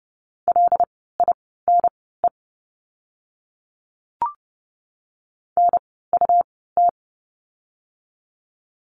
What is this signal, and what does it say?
Morse code, heard as a keyed tone, about 30 wpm, spaced as for 17 wpm, 700 Hz. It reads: LINE NUT
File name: Lesson 17f - Letter u - Words 30wpm